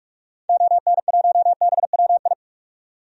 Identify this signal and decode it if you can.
Morse code, heard as a keyed tone, about 45 wpm, 700 Hz. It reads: ON1BWI
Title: DXCC-100 Call Signs - 6 Characters - 45wpm